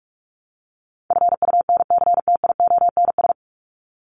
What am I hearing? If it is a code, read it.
Morse code, heard as a keyed tone, about 44 wpm, 700 Hz. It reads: FUNCTIONS
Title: ES:S Suffix - Top 100 Words with Base Words 45wpm